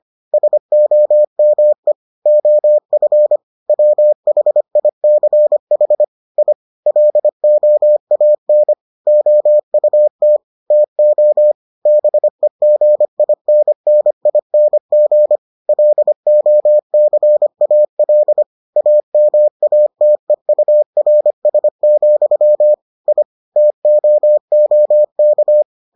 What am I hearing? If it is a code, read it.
Morse code, heard as a keyed tone, about 25 wpm, 600 Hz. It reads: SOME OF WHICH I LOAN OUT TO BEGINNING LOCAL AMATEURS, I TOOK